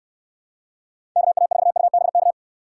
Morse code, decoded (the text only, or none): KA4UBK